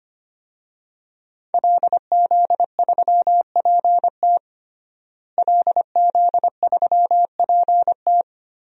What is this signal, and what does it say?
Morse code, heard as a keyed tone, about 25 wpm, 700 Hz. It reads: LZ3PT LZ3PT
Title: DXCC-100 Call Signs - 5 Characters - Repeated 3x - 25wpm